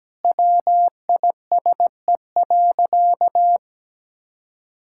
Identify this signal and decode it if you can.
Morse code, heard as a keyed tone, about 17 wpm, 700 Hz. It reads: WISE.
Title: Sentences from Top 1600 Words 17wpm